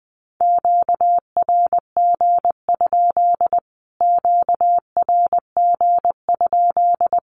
Morse code, heard as a keyed tone, about 20 wpm, 700 Hz. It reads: QRG? QRG?
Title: Q-Codes Repeated 3x 20wpm